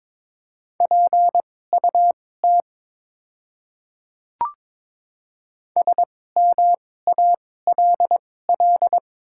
Morse code, heard as a keyed tone, about 22 wpm, spaced as for 16 wpm, 700 Hz. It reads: PUT SMALL
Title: 2x Character Spacing - Top 200 Words - 22wpm